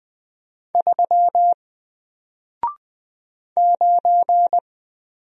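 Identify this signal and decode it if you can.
Morse code, heard as a keyed tone, about 20 wpm, 700 Hz. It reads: 3 9
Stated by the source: Single letter and number Morse drill at 20 wpm, from N4LCD.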